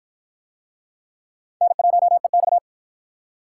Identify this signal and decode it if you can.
Morse code, heard as a keyed tone, about 53 wpm, 700 Hz. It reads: N1EX